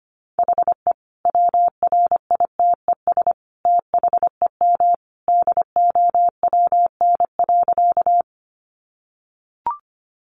Morse code, heard as a keyed tone, about 25 wpm, 700 Hz. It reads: HE WRITES THEM DOWN.
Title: Additional Easy Phrases 25wpm